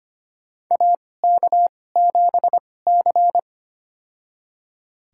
Morse code, heard as a keyed tone, about 25 wpm, spaced as for 18 wpm, 700 Hz. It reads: AK7C
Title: USA Call Signs - 4 Characters - 2x Character Spacing - 25wpm